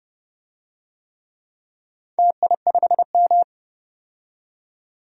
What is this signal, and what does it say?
Morse code, heard as a keyed tone, about 30 wpm, 700 Hz. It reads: TI5M
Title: DXCC-100 Call Signs - 4 Characters - 30wpm